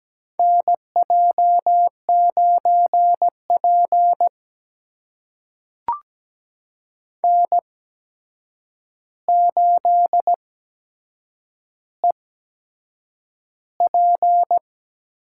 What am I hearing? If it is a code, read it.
Morse code, heard as a keyed tone, about 17 wpm, spaced as for 7 wpm, 700 Hz. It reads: NJ9P N8EP
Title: USA Call Signs - 4 Characters - 8x Character Spacing - 17wpm